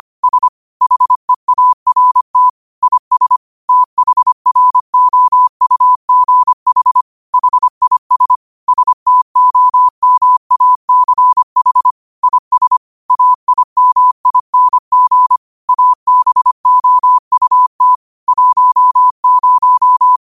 Morse code, heard as a keyed tone, about 25 wpm, 1000 Hz. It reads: S HEART IS THROUGH HIS STOMACH IS AIMING ABOUT 10
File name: qotdfiles / 25wpm